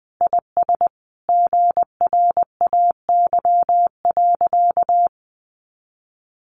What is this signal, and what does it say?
Morse code, heard as a keyed tone, about 20 wpm, 700 Hz. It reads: IS GRAY.
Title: Additional Easy Phrases 20wpm